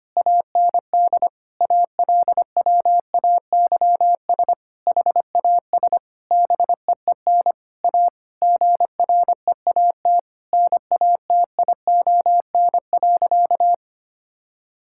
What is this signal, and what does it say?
Morse code, heard as a keyed tone, about 25 wpm, 700 Hz. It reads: AND ALWAYS HAS BEEN A GREAT NATION.
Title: Sentences from Top 1000 Words - Review Part 6 - 25wpm